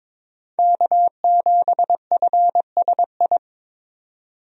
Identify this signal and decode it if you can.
Morse code, heard as a keyed tone, about 22 wpm, 700 Hz. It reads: K7FSI
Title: USA Call Signs - 5 Characters - 22wpm